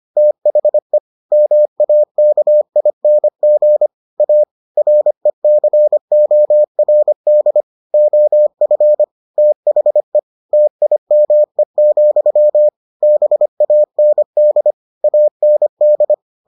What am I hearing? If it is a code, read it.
Morse code, heard as a keyed tone, about 25 wpm, 600 Hz. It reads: THE MAKING A RECORD OF THE TIME, BAND AND